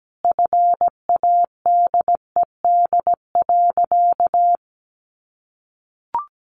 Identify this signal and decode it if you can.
Morse code, heard as a keyed tone, about 17 wpm, 700 Hz. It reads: FADED.
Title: A Sound Of Thunder 17wpm